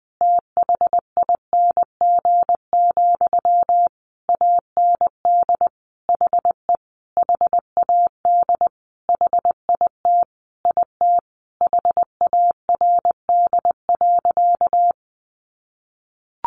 Morse code, heard as a keyed tone, about 20 wpm, 700 Hz. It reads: THING, AND HE HAD HIT IT HARD.
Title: Sentences from Top 1200 Words 20wpm